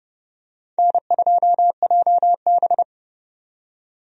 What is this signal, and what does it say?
Morse code, heard as a keyed tone, about 31 wpm, 700 Hz. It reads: N2JB
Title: USA Call Signs - 4 Characters - 30wpm